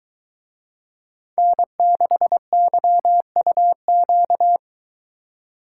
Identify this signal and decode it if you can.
Morse code, heard as a keyed tone, about 22 wpm, 700 Hz. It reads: N6YUQ